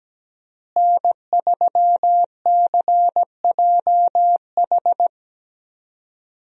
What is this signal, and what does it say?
Morse code, heard as a keyed tone, about 17 wpm, 700 Hz. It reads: N3CJH
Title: USA Call Signs - 5 Characters - Repeated 2x - 17wpm